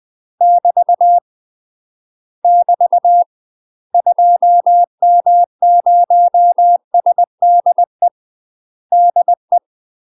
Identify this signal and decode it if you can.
Morse code, heard as a keyed tone, about 20 wpm, 700 Hz. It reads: <BT> <BT> 2M0SDE DE
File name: UK MASTER QSOs TRAINING / 20WPM / UK MASTER QSOs 0039